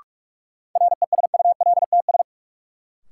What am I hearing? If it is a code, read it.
Morse code, heard as a keyed tone, about 46 wpm, 700 Hz. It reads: RESULTS
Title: Sets of 4 Words - New Word Drill - 45wpm